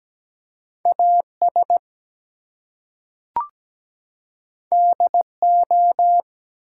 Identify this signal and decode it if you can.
Morse code, heard as a keyed tone, about 17 wpm, 700 Hz. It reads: AS DO
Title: Two Letter Words - 17wpm